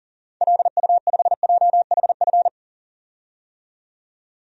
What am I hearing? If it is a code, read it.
Morse code, heard as a keyed tone, about 40 wpm, 700 Hz. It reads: LU5JHF
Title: DXCC-100 Call Signs - 6 Characters - 40wpm